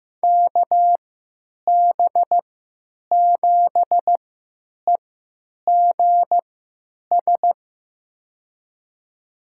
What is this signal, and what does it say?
Morse code, heard as a keyed tone, about 15 wpm, spaced as for 9 wpm, 700 Hz. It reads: KB7EGS
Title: USA Call Signs - 6 Characters - 3x Character Spacing - 15wpm